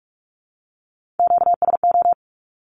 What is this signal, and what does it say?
Morse code, heard as a keyed tone, about 46 wpm, 700 Hz. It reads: QSO
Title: Q-Codes with Spoken Meaning - 45wpm